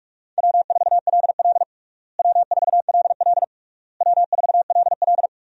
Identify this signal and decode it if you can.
Morse code, heard as a keyed tone, about 46 wpm, 700 Hz. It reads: W4LL W4LL W4LL
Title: USA Call Signs - 4 Characters - Repeated 3x - 45wpm